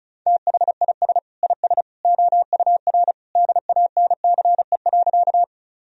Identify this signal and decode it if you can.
Morse code, heard as a keyed tone, about 34 wpm, 700 Hz. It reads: THIS IS OUR DANCE.
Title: Sentences from Top 1000 Words - Review Part 4 - 35wpm